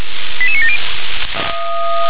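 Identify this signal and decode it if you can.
Five-tone selective call (ZVEI), heard as quick runs of five tones, tones of about 70 ms. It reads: 88970
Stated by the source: Five-tone ZVEI selective call received by SDR.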